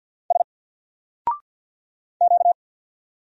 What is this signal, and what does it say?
Morse code, heard as a keyed tone, about 50 wpm, 700 Hz. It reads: S Q
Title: Single Character (Letter:Number:Punctuation:Prosign) 50wpm